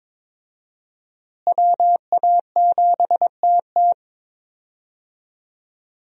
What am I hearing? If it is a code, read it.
Morse code, heard as a keyed tone, about 21 wpm, 700 Hz. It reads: WA7TT
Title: USA Call Signs - 5 Characters - Repeated 3x - 22wpm